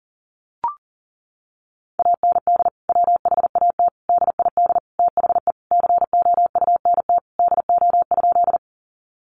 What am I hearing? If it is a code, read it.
Morse code, heard as a keyed tone, about 40 wpm, 700 Hz. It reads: AND WHAT DID THE COUNT DO?
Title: Sentences from Top 1500 Words 40wpm